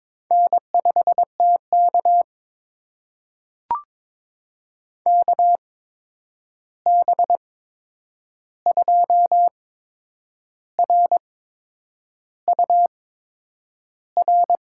Morse code, heard as a keyed tone, about 21 wpm, spaced as for 8 wpm, 700 Hz. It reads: N5TK KB2RUR
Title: USA Call Signs - 8x Character Spacing - 22wpm